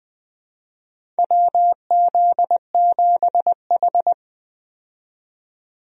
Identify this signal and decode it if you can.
Morse code, heard as a keyed tone, about 21 wpm, 700 Hz. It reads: WZ7H
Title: USA Call Signs - 4 Characters - 20wpm